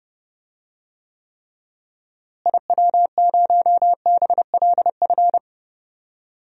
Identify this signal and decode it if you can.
Morse code, heard as a keyed tone, about 30 wpm, 700 Hz. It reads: IW0BLF
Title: DXCC-100 Call Signs - 6 Characters - Repeated 3x - 30wpm